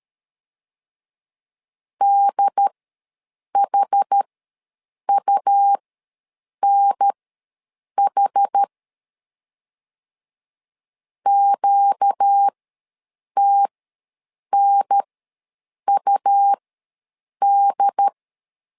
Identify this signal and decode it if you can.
Morse code, heard as a keyed tone, about 13 wpm, spaced as for 7 wpm, 800 Hz. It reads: DHUNH QTNUD
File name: LEZIONI / 4 Q ws 12wpm 1912